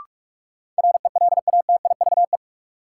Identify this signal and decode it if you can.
Morse code, heard as a keyed tone, about 46 wpm, 700 Hz. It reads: RELATIVE
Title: Sentences from Top 1400 Words - New Word Drill - 45wpm